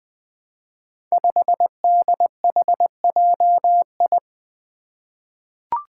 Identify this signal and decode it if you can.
Morse code, heard as a keyed tone, about 20 wpm, 700 Hz. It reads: 5DHJI
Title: Five Challenging Letter-Number Combinations 20wpm